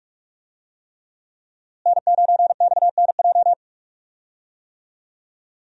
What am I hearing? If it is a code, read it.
Morse code, heard as a keyed tone, about 44 wpm, 700 Hz. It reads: N9XNJ
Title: USA Call Signs - 5 Characters - Repeated 3x - 45wpm